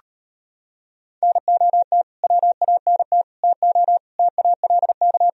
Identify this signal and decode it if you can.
Morse code, heard as a keyed tone, about 33 wpm, 700 Hz. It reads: NOT WANT TO TALK